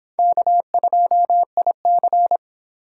Morse code, heard as a keyed tone, about 25 wpm, 700 Hz. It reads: K2IC